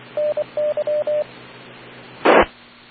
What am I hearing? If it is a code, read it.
Morse code, heard as a keyed tone, about 24 wpm, 600 Hz, with radio noise. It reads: NY T